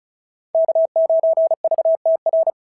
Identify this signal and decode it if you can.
Morse code, heard as a keyed tone, about 34 wpm, 690 Hz. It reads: K9VTR
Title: SOTA QSO Practice 35wpm